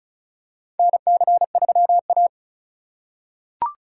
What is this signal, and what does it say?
Morse code, heard as a keyed tone, about 36 wpm, 700 Hz. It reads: NC3A
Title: USA Call Signs - 2x Character Spacing - 35wpm